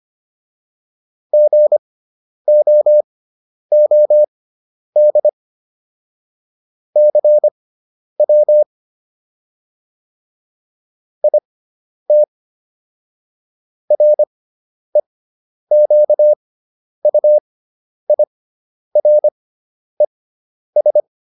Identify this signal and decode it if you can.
Morse code, heard as a keyed tone, about 25 wpm, spaced as for 10 wpm, 600 Hz. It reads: GOOD CW IT REQUIRES